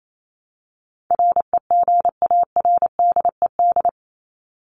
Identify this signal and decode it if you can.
Morse code, heard as a keyed tone, about 28 wpm, 700 Hz. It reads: REGARDED
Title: ED Suffix - Top 100 Words with Base Words 28wpm